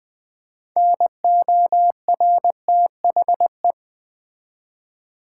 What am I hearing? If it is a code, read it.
Morse code, heard as a keyed tone, about 20 wpm, 700 Hz. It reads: NORTHE
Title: Build A Word - 3 Letters At A Time - 8-Letter Words - 20wpm